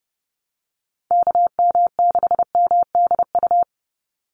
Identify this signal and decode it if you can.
Morse code, heard as a keyed tone, about 30 wpm, 700 Hz. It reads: KM6MDU
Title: USA Call Signs 30wpm